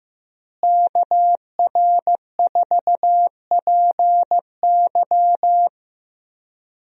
Morse code, heard as a keyed tone, about 15 wpm, 700 Hz. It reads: KR4PY
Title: USA Call Signs - 5 Characters - Repeated 2x - 15wpm